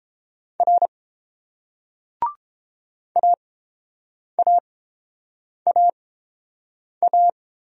Morse code, heard as a keyed tone, about 30 wpm, spaced as for 7 wpm, 690 Hz. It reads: R AAAA